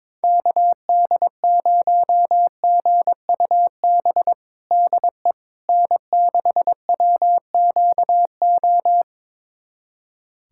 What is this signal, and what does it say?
Morse code, heard as a keyed tone, about 21 wpm, 700 Hz. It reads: KD0GUB DE N6WQO
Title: USA Callsign DE Callsign 22wpm